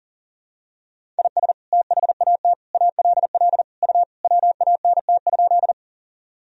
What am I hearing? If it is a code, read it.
Morse code, heard as a keyed tone, about 40 wpm, 700 Hz. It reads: IS THAT ALL U WANT?
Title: Sentences from Top 200 Words and Abbreviations 40wpm